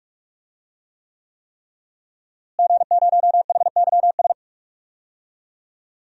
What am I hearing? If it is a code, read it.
Morse code, heard as a keyed tone, about 46 wpm, 700 Hz. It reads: G0HYS
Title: DXCC-100 Call Signs - 5 Characters - Repeated 3x - 45wpm